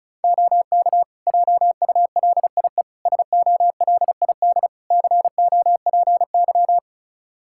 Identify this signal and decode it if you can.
Morse code, heard as a keyed tone, about 36 wpm, 700 Hz. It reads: OK JULIE SOLID COPY
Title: Practice QSO - Exchange 3 - More Info - 35wpm